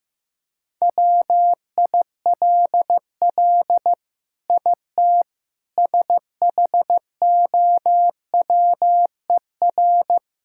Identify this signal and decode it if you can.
Morse code, heard as a keyed tone, about 16 wpm, 700 Hz. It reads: WILL IT SHOWER
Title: Sentences from Top 1600 Words 15wpm